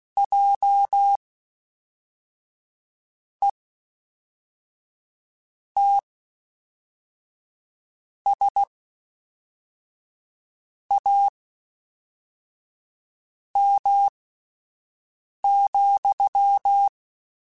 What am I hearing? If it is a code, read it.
Morse code, heard as a keyed tone, about 16 wpm, spaced as for 4 wpm, 800 Hz. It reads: JETSAM,
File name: Misc / Misc-mp3s / WordPairs sp @5